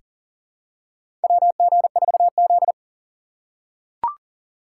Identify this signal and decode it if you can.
Morse code, heard as a keyed tone, about 40 wpm, 700 Hz. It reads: WG4Z
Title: USA Call Signs - 4x Character Spacing - 40wpm